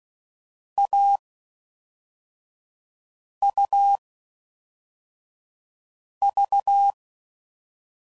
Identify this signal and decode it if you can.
Morse code, heard as a keyed tone, about 16 wpm, spaced as for 4 wpm, 800 Hz. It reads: AUV
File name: Combinations / Difficult Combos / AUV AUV4